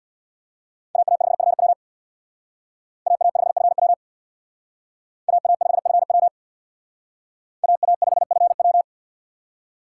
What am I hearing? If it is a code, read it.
Morse code, heard as a keyed tone, about 62 wpm, 690 Hz. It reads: AA5FW AA5FW AA5FW AA5FW
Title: Speed Racing - USA 2x2 Call Signs 50wpm